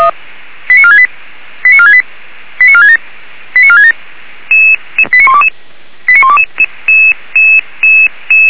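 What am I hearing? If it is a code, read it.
Five-tone selective call (ZVEI), heard as quick runs of five tones, tones of about 70 ms. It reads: 89468 89468 89467 89467 89122 89122
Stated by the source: Five-tone ZVEI selective call received on SDR, from the audio archive of Feuerwehr Orlamünde.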